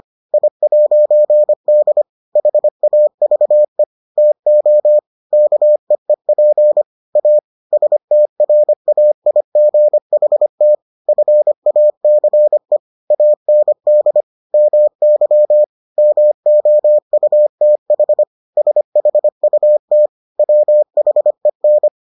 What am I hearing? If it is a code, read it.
Morse code, heard as a keyed tone, about 25 wpm, 600 Hz. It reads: I'D HAVE TO KEEP A STRAIGHT FACE AND MY MOUTH SHUT WHEN